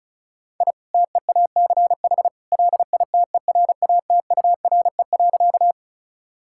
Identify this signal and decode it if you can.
Morse code, heard as a keyed tone, about 34 wpm, 700 Hz. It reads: I TEACH LITERATURE.
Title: Sentences from Top 1000 Words - Review Part 6 - 35wpm